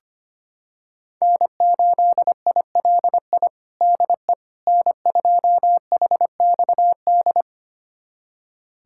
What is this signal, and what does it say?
Morse code, heard as a keyed tone, about 25 wpm, 700 Hz. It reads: N8ILI DE N2HXD
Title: USA Callsign DE Callsign - 5 Characters - 25wpm